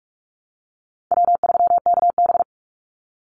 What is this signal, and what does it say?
Morse code, heard as a keyed tone, about 44 wpm, 700 Hz. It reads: W3KB
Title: Easy Call Signs Repeated 2x - 45wpm